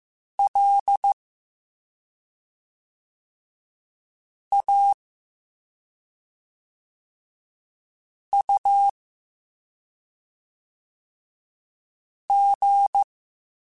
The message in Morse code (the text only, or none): L A U G